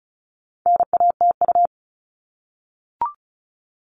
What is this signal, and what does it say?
Morse code, heard as a keyed tone, about 34 wpm, 700 Hz. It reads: NATU
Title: All Cut Numbers - 4-Digit 35wpm